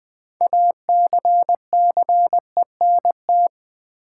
Morse code, heard as a keyed tone, about 20 wpm, 700 Hz. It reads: ACCENT
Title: 6 Letter Words - First 3 Letters Given - 20wpm